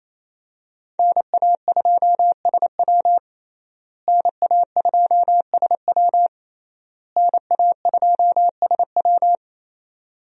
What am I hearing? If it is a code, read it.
Morse code, heard as a keyed tone, about 28 wpm, 700 Hz. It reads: NA2SW NA2SW NA2SW
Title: USA Call Signs - 5 Characters - Repeated 3x - 28wpm